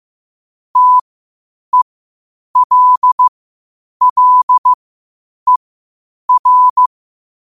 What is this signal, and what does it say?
Morse code, heard as a keyed tone, about 15 wpm, spaced as for 8 wpm, 1000 Hz. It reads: TELLER
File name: qotdfiles / 8wpm